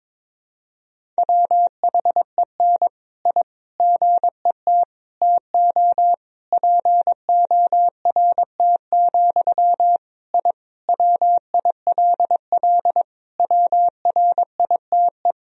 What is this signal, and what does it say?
Morse code, heard as a keyed tone, about 22 wpm, 700 Hz. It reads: WHEN I GET TO PORT, I WILL WRITE
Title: Sentences from Top 1000 Words - Review Part 1 - 22wpm